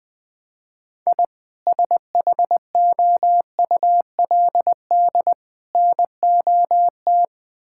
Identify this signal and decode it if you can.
Morse code, heard as a keyed tone, about 20 wpm, 700 Hz. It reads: I SHOULD NOT
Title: Sentences from Top 1000 Words - Review Part 5 - 20wpm